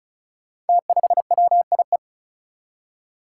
Morse code, heard as a keyed tone, about 36 wpm, 700 Hz. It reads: T5WIE
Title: Five Challenging Letter-Number Combinations 35wpm